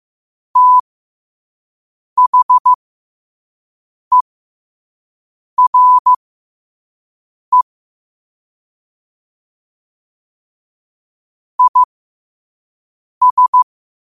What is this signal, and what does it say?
Morse code, heard as a keyed tone, about 15 wpm, spaced as for 5 wpm, 1000 Hz. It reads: THERE IS
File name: qotdfiles / 5wpm